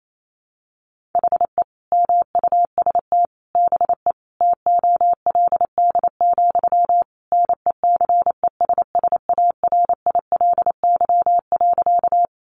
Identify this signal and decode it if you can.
Morse code, heard as a keyed tone, about 28 wpm, 700 Hz. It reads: HE MUST BE TOLD, NECESSARILY.
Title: Sentences from Top 1400 Words 28wpm